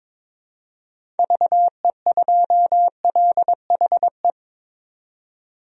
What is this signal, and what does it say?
Morse code, heard as a keyed tone, about 22 wpm, 700 Hz. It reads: VE2LHE
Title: DXCC-100 Call Signs - 6 Characters - 22wpm